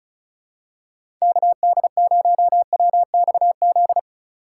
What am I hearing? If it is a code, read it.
Morse code, heard as a keyed tone, about 36 wpm, 700 Hz. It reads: KD0WXZ
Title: USA Call Signs - 6 Characters - 3x Character Spacing - 35wpm